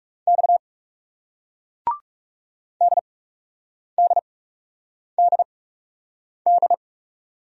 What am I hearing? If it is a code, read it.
Morse code, heard as a keyed tone, about 39 wpm, spaced as for 8 wpm, 700 Hz. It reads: X DDDD